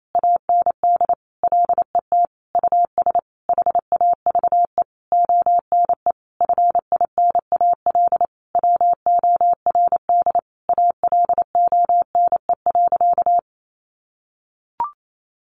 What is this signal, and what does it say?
Morse code, heard as a keyed tone, about 28 wpm, 700 Hz. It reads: AND LET US HAVE ONE FINAL WORD ALONE.
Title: Sentences from Top 1000 Words 28wpm